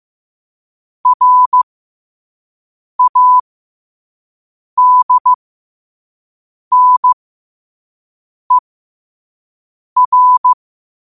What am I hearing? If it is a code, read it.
Morse code, heard as a keyed tone, about 15 wpm, spaced as for 5 wpm, 1000 Hz. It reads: RADNER